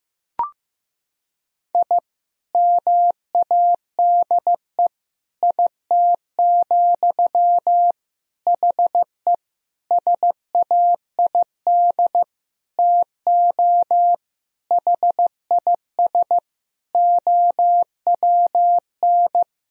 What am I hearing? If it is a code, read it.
Morse code, heard as a keyed tone, about 15 wpm, 700 Hz. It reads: I MADE IT, HE SAID TO HIS OWN
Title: Sentences from Top 1000 Words - Review Part 1 - 15wpm